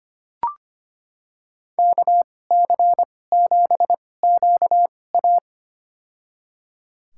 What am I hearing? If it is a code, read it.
Morse code, heard as a keyed tone, about 25 wpm, spaced as for 18 wpm, 700 Hz. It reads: KC7QA
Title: USA Call Signs - 5 Characters - 2x Character Spacing - 25wpm